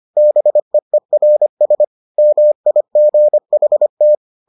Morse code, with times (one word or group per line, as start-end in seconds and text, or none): 0.2-1.8 BEERS
2.2-4.2 MIGHT